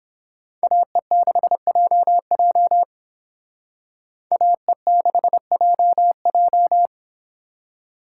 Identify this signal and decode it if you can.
Morse code, heard as a keyed tone, about 26 wpm, 700 Hz. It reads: AE6JJ AE6JJ
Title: Speed Racing - USA 2x2 Call Signs 22wpm